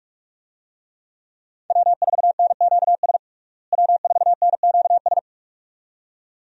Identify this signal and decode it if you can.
Morse code, heard as a keyed tone, about 44 wpm, 700 Hz. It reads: W4NQS W4NQS
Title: USA Call Signs - 5 Characters - Repeated 2x - 45wpm